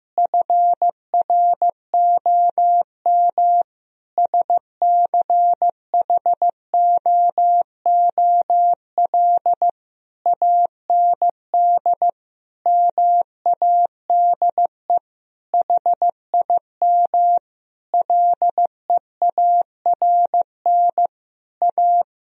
Morse code, heard as a keyed tone, about 16 wpm, 700 Hz. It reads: FROM SCHOOL AND MADE HIM LEARN A
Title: Sentences from Top 1000 Words - Review Part 4 - 15wpm